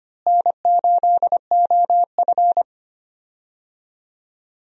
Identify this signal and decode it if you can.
Morse code, heard as a keyed tone, about 25 wpm, 700 Hz. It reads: N8OF
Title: USA Call Signs - 4 Characters - Repeated 3x - 25wpm